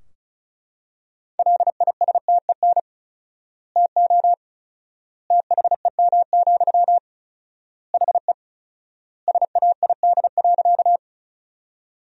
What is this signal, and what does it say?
Morse code, heard as a keyed tone, about 34 wpm, 700 Hz. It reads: LISTEN TO THEM, HE SAID.
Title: 4x Word-Spacing - Sentences from Top 400 Words - 35wpm